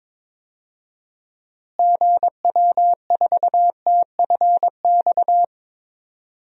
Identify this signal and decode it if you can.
Morse code, heard as a keyed tone, about 21 wpm, 700 Hz. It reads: GW4TFX
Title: DXCC-100 Call Signs - 6 Characters - Repeated 2x - 22wpm